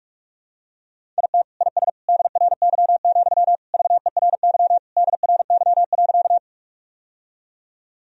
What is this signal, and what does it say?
Morse code, heard as a keyed tone, about 45 wpm, 700 Hz. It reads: IT IS DRY, VERY DRY.
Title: Sentences from Top 1000 Words - Review Part 3 - 45wpm